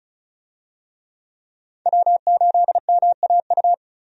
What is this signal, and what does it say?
Morse code, heard as a keyed tone, about 36 wpm, 700 Hz. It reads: W8MAU
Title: USA Call Signs - 5 Characters - 35wpm